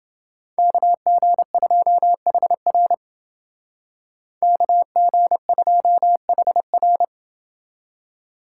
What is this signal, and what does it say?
Morse code, heard as a keyed tone, about 28 wpm, 700 Hz. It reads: KG2HR KG2HR